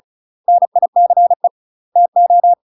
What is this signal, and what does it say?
Morse code, heard as a keyed tone, about 36 wpm, 700 Hz. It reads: NICE TO
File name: UK MASTER QSOs TRAINING / 35WPM / UK MASTER QSOs 0010